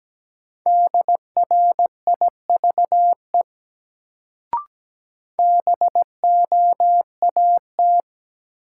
Morse code, heard as a keyed tone, about 17 wpm, 700 Hz. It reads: DRIVE BOAT
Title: Sentences from Top 500 Words - New Word Drill - 17wpm